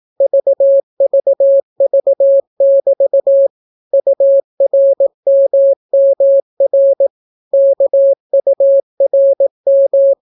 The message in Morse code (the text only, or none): VVV<BT> URMMR KURM